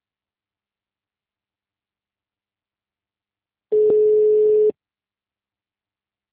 Ringback tone (Europe, 425 Hz).
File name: out- -2001- - - .73656